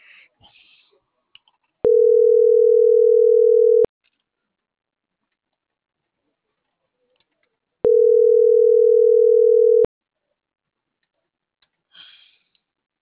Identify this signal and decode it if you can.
Ringback tone (North America), 440 + 480 Hz.